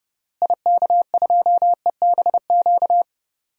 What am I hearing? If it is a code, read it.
Morse code, heard as a keyed tone, about 30 wpm, 700 Hz. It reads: IK2EBQ
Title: DXCC-100 Call Signs - 6 Characters - Repeated 3x - 30wpm